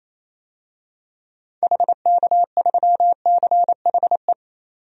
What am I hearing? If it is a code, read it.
Morse code, heard as a keyed tone, about 28 wpm, 700 Hz. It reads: HK3CHE
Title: DXCC-100 Call Signs - 6 Characters - 28wpm